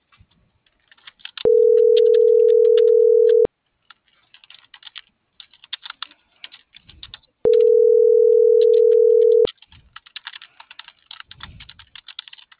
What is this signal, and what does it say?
Ringback tone (North America), 440 + 480 Hz.